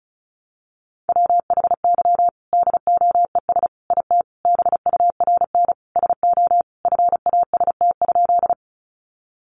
Morse code, heard as a keyed tone, about 34 wpm, 700 Hz. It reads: WHY DOES IT BURN SO FAST?
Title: Sentences from Top 1300 Words 35wpm